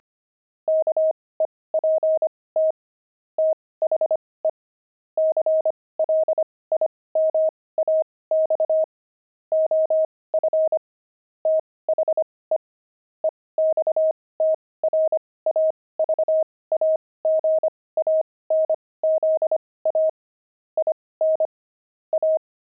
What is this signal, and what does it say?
Morse code, heard as a keyed tone, about 25 wpm, spaced as for 18 wpm, 630 Hz. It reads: KEPT THE CLIMAX OF THE EXTRAVAGANZA IN A
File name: pangram-fs-18-cs-25-jelly